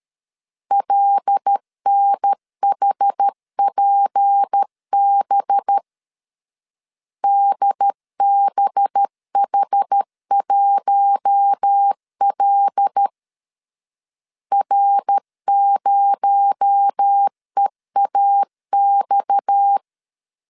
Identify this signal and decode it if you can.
Morse code, heard as a keyed tone, about 13 wpm, 800 Hz. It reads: LNHPB DBH1L R0EAX
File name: CW / DET / 1 6 1500C S A 12wpm 3440